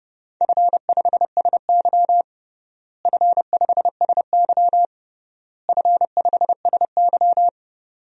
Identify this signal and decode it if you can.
Morse code, heard as a keyed tone, about 30 wpm, 700 Hz. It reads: F5SY F5SY F5SY
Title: DXCC-100 Call Signs - 4 Characters - Repeated 3x - 30wpm